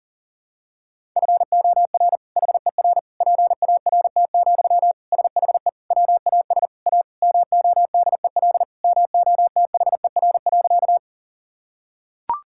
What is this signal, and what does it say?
Morse code, heard as a keyed tone, about 40 wpm, 700 Hz. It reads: FOR HER PART, SHE WAS A MODEL MOTHER.
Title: Sentences from Top 1000 Words - Review Part 5 - 40wpm